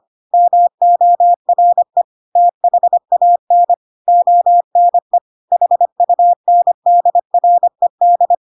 Morse code, heard as a keyed tone, about 25 wpm, 700 Hz. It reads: MORE THAN ONE HUNDRED
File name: Pinocchio / Pinocchio ch4 25 25 ik1hge